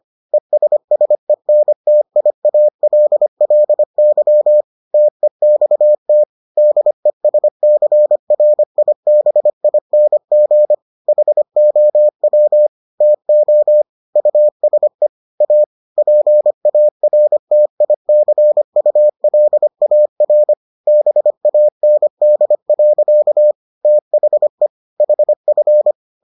Morse code, heard as a keyed tone, about 25 wpm, 600 Hz. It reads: ESSENTIALLY TEXT DESCRIBING HOW TO USE A PARTICULAR BAND. THE HF